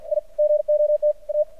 Morse code, heard as a keyed tone, about 55 wpm, 600 Hz. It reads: I GOT A
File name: archived / CRSnet